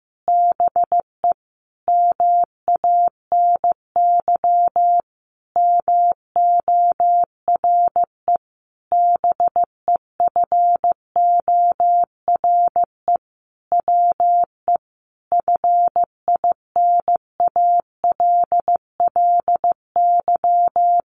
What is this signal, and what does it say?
Morse code, heard as a keyed tone, about 15 wpm, 700 Hz. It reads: BE MANY MORE BEFORE WE FINALLY
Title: Sentences from Top 1500 Words 15wpm